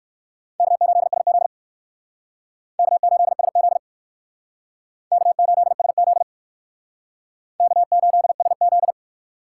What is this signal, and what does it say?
Morse code, heard as a keyed tone, about 64 wpm, 700 Hz. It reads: K8SZ K8SZ K8SZ K8SZ